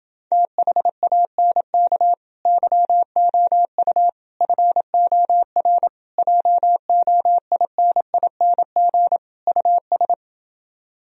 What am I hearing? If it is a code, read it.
Morse code, heard as a keyed tone, about 30 wpm, 700 Hz. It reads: THANK YOU FOR JOINING US